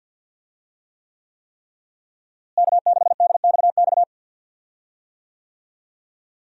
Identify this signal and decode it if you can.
Morse code, heard as a keyed tone, about 50 wpm, 700 Hz. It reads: K6DXX
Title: USA Call Signs - 5 Characters - Repeated 3x - 50wpm